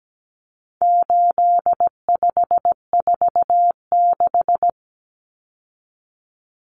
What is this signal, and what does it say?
Morse code, heard as a keyed tone, about 17 wpm, 700 Hz. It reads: 8546
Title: Common Cut Numbers - 4-Digit 17wpm